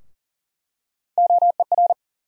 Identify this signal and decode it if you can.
Morse code, heard as a keyed tone, about 40 wpm, 700 Hz. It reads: OER
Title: USA Call Sign Postfixes - 3 Letters 40wpm